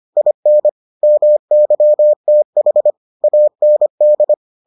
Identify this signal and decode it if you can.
Morse code, heard as a keyed tone, about 25 wpm, 600 Hz. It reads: IN MYTH AND